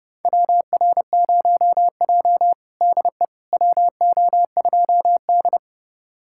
Morse code, heard as a keyed tone, about 31 wpm, 700 Hz. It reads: WR0J DE WO2D
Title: USA Callsign DE Callsign - 4 Characters - 30wpm